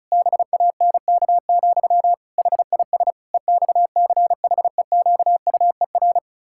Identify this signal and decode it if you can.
Morse code, heard as a keyed tone, about 34 wpm, 700 Hz. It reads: BANK, HIS EXCHEQUER
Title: Pangram Challenge v2 - 35wpm